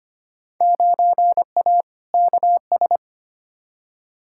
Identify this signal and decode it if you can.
Morse code, heard as a keyed tone, about 25 wpm, 700 Hz. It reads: 9A KS
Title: Field Day Exchanges Encore - 25wpm